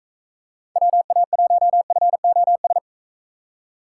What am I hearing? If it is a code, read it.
Morse code, heard as a keyed tone, about 44 wpm, 700 Hz. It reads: WA1FOS